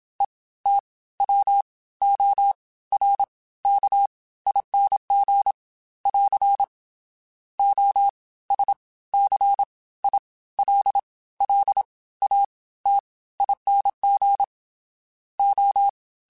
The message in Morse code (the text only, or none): ETWORKING <AR> OSCILLATING O